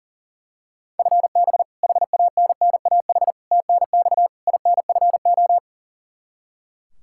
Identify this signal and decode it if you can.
Morse code, heard as a keyed tone, about 40 wpm, 700 Hz. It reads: FB HANNAH TNX INFO
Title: Practice QSO - Exchange 3 - More Info - 40wpm